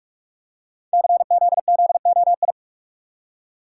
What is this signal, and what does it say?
Morse code, heard as a keyed tone, about 44 wpm, 700 Hz. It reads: CZZOI
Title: Code Groups - 5 Letters - 45wpm